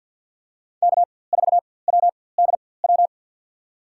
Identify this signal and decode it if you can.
Morse code, heard as a keyed tone, about 47 wpm, spaced as for 23 wpm, 700 Hz. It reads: K4WDW
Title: USA Call Signs - 4x Character Spacing - 50wpm